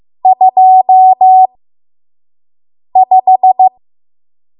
Morse code, heard as a keyed tone, about 15 wpm, spaced as for 5 wpm, 750 Hz. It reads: 25
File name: Morse / 05WPM